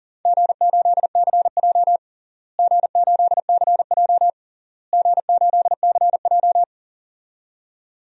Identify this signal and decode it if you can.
Morse code, heard as a keyed tone, about 40 wpm, 700 Hz. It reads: G8CJ G8CJ G8CJ
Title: DXCC-100 Call Signs - 4 Characters - Repeated 3x - 40wpm